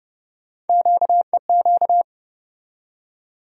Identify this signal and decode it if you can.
Morse code, heard as a keyed tone, about 30 wpm, 700 Hz. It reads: QEQ
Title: USA Call Sign Postfixes - 3 Letters 30wpm